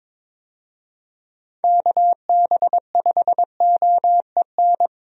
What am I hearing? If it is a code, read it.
Morse code, heard as a keyed tone, about 21 wpm, 700 Hz. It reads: KB5OEN
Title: USA Call Signs - 6 Characters - 22wpm